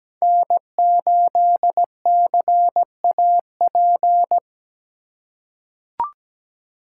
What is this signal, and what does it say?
Morse code, heard as a keyed tone, about 17 wpm, 700 Hz. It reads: N8CAP
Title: USA Call Signs - 4x Character Spacing - 17wpm